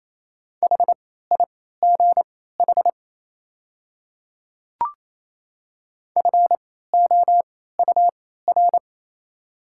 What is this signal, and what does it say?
Morse code, heard as a keyed tone, about 28 wpm, spaced as for 16 wpm, 690 Hz. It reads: HIGH FOUR
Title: 3x Character Spacing - Top 500 Words - 28wpm